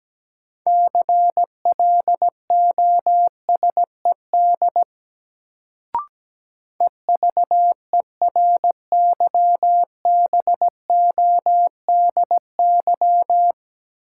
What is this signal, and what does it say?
Morse code, heard as a keyed tone, about 17 wpm, 700 Hz. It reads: CLOSED EVERYBODY
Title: Sentences from Top 1400 Words - New Word Drill - 17wpm